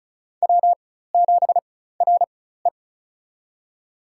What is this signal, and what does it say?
Morse code, heard as a keyed tone, about 36 wpm, spaced as for 16 wpm, 700 Hz. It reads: W7RE
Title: USA Call Signs - 4 Characters - 4x Character Spacing - 35wpm